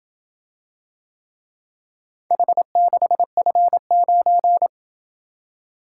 Morse code, heard as a keyed tone, about 26 wpm, 700 Hz. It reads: H6F9